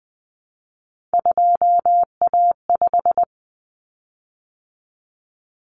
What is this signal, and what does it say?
Morse code, heard as a keyed tone, about 20 wpm, 700 Hz. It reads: 2A5
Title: Common Cut Numbers - 3-Digit 20wpm